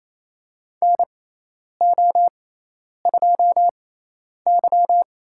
Morse code, heard as a keyed tone, about 28 wpm, spaced as for 10 wpm, 700 Hz. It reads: NO2Y
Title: USA Call Signs - 6x Character Spacing - 28wpm